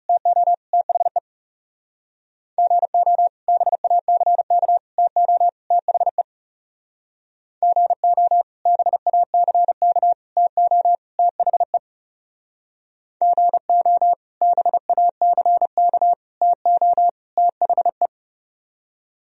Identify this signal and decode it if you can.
Morse code, heard as a keyed tone, about 36 wpm, 700 Hz. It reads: TO THE GO BACK TO THE GO BACK TO THE GO BACK TO THE
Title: Speed-Racing - Sets of 4 Words 30wpm